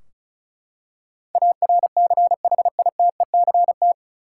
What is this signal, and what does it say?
Morse code, heard as a keyed tone, about 36 wpm, 700 Hz. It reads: ARCHITECT
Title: Practice QSO - Exchange 3 - More Info - 35wpm